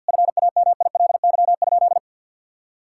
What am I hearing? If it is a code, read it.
Morse code, heard as a keyed tone, about 52 wpm, 700 Hz. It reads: FAMILY?
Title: Sentences from Top 1000 Words - Review Part 5 - 50wpm